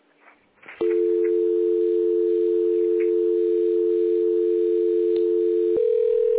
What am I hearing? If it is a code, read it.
Dial tone (North America), 350 + 440 Hz.
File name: 1040.3000.b3de8914-0035-431f-8535